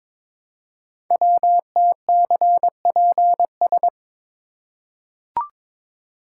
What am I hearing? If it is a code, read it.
Morse code, heard as a keyed tone, about 22 wpm, 700 Hz. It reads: WTCPS
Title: Code Groups - 5 Letters - 22wpm